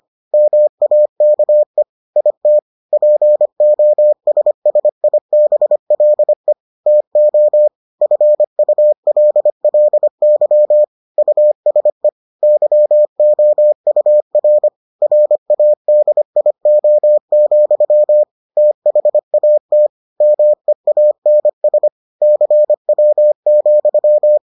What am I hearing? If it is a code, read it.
Morse code, heard as a keyed tone, about 25 wpm, 600 Hz. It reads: MAKE IT POSSIBLE TO FULLY USE YOUR RADIO, THAT MEANS CW,